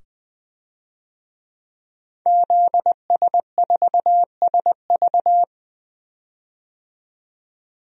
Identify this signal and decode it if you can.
Morse code, heard as a keyed tone, about 20 wpm, 700 Hz. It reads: ZS4SV
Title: DXCC-100 Call Signs - 5 Characters - 20wpm